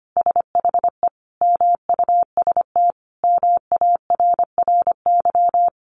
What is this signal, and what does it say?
Morse code, heard as a keyed tone, about 25 wpm, 700 Hz. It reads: SHE MUST MARRY
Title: Sentences from Top 1200 Words 25wpm